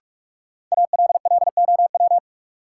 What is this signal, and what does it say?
Morse code, heard as a keyed tone, about 46 wpm, 700 Hz. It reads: ALLOW
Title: Sentences from Top 700 Words - New Word Drill - 45wpm